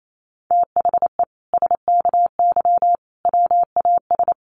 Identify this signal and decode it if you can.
Morse code, heard as a keyed tone, about 28 wpm, 700 Hz. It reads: THE SKY WAS
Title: Sentences from Top 1200 Words 28wpm